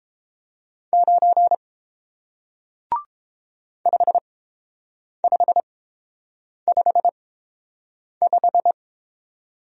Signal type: Morse code

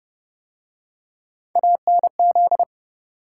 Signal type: Morse code

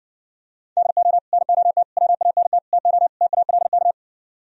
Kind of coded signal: Morse code

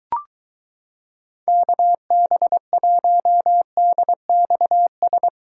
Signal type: Morse code

